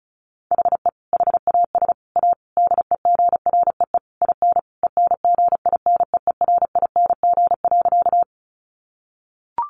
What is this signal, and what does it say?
Morse code, heard as a keyed tone, about 34 wpm, 700 Hz. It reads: HE HAS A DEGREE IN ENGINEERING.
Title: Sentences from Top 1500 Words 35wpm